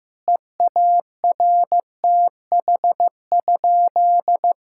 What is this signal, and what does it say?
Morse code, heard as a keyed tone, about 15 wpm, 700 Hz. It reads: EARTH?
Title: Sentences from Top 1000 Words - Review Part 5 - 15wpm